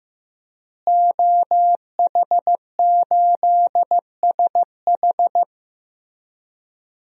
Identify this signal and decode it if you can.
Morse code, heard as a keyed tone, about 15 wpm, 700 Hz. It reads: OH8SH
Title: DXCC-100 Call Signs - 5 Characters - Repeated 3x - 15wpm